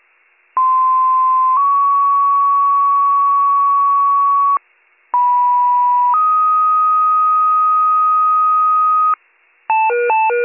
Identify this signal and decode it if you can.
Two-tone page, heard as a one-second tone, then a three-second tone, about 1040 then 1120 Hz, about 980 then 1250 Hz. It is heard from 0.6 to 9.1 s.